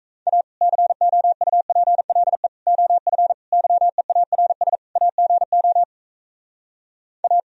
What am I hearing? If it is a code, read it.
Morse code, heard as a keyed tone, about 38 wpm, 700 Hz. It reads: A COUPLE OF YEARS AGO A